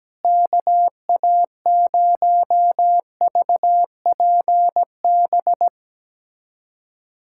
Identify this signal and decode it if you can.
Morse code, heard as a keyed tone, about 17 wpm, 700 Hz. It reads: KA0VPB
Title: USA Callsign DE Callsign - 6 Characters - 17wpm